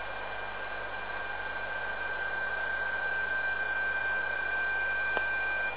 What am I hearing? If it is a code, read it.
PSK31, about 31.25 baud.